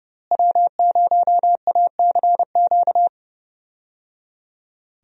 Morse code, heard as a keyed tone, about 30 wpm, 700 Hz. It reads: W0ACQ
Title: USA Call Signs - 5 Characters - Repeated 3x - 30wpm